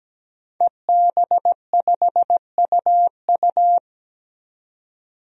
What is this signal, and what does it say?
Morse code, heard as a keyed tone, about 17 wpm, 700 Hz. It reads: EB5UU
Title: DXCC-100 Call Signs - 5 Characters - 17wpm